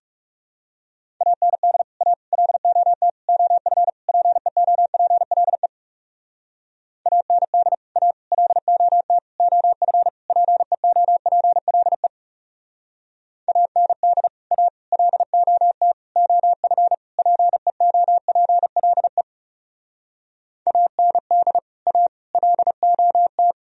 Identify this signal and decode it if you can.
Morse code, heard as a keyed tone, about 35 wpm, 700 Hz. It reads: AND A LOT OF PEOPLE AND A LOT OF PEOPLE AND A LOT OF PEOPLE AND A LOT
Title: Speed-Racing - Sets of 5 Words 30wpm